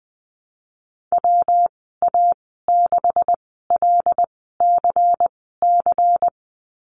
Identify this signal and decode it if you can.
Morse code, heard as a keyed tone, about 20 wpm, spaced as for 15 wpm, 700 Hz. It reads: WA6LCC
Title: 154 Lesson 25g - Letter p - Call Signs 20wpm